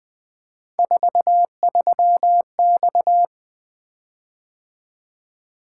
Morse code, heard as a keyed tone, about 20 wpm, 700 Hz. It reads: 43X
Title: Three Letter-Number Combinations - 20wpm